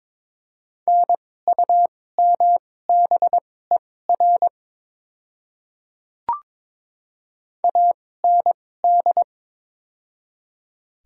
Morse code, heard as a keyed tone, about 22 wpm, spaced as for 16 wpm, 690 Hz. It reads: NUMBER AND
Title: 2x Character Spacing - Top 100 Words - 22wpm